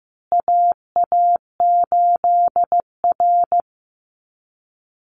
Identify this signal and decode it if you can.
Morse code, heard as a keyed tone, about 15 wpm, 700 Hz. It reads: AA8R
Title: Call Signs Repeated 3x - 15wpm